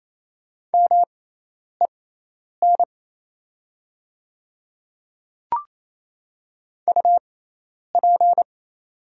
Morse code, heard as a keyed tone, about 28 wpm, spaced as for 10 wpm, 700 Hz. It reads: MEN UP
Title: 6x Character Spacing - Top 300 Words - 28wpm